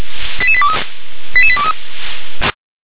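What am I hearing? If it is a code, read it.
Five-tone selective call (ZVEI), heard as quick runs of five tones, tones of about 70 ms. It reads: 88823 88823
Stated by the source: ZVEI five-tone selective call 88836 received with an SDR.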